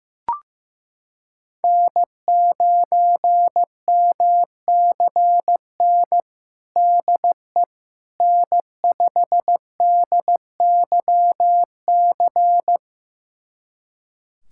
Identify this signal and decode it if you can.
Morse code, heard as a keyed tone, about 15 wpm, 700 Hz. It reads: N9MCN DE N5DYC
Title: USA Callsign DE Callsign - 5 Characters - 15wpm